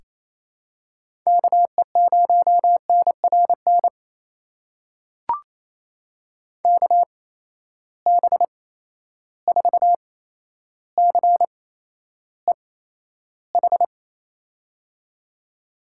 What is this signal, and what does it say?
Morse code, heard as a keyed tone, about 28 wpm, spaced as for 12 wpm, 700 Hz. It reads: KE0NRN KB4CEH
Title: USA Call Signs - 6 Characters - 8x Character Spacing - 28wpm